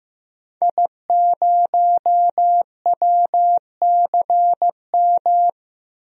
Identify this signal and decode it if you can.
Morse code, heard as a keyed tone, about 15 wpm, 700 Hz. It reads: I0WCM
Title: DXCC-100 Call Signs - 5 Characters - 15wpm